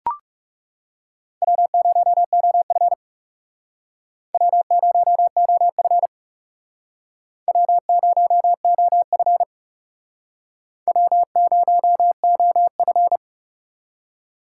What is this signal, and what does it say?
Morse code, heard as a keyed tone, about 40 wpm, 700 Hz. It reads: W0OF W0OF W0OF W0OF